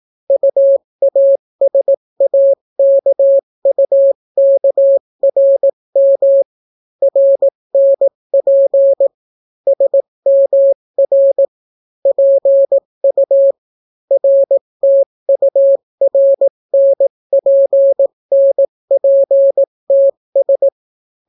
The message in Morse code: UASAKUKRM RNP SMR PU RTURNPNPTS

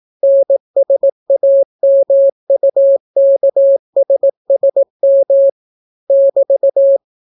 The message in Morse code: NSAMUKSSM <BT>

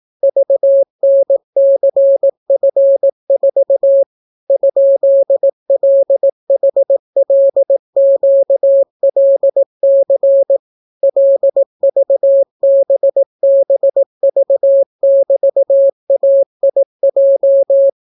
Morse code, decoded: VNCF4 ?LHLQLC LVBBV<BT>AIJ

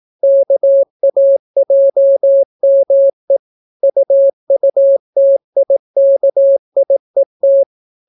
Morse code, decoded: KAJME UUTIKIET

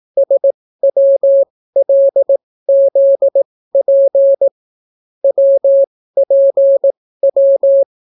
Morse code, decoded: SWLZP WPW

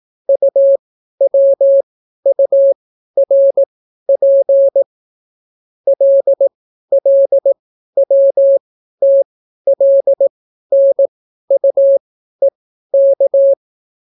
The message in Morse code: UWURP LLWTLNUEK

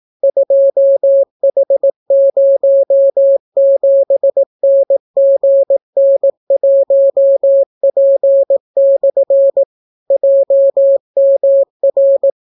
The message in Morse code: 2H07NGN1P/ JMR

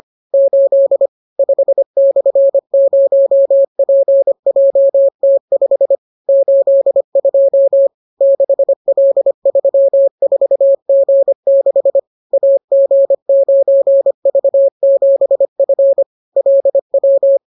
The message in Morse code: 8 5/0PJT5 82 6L34G6 AG9V7F LW

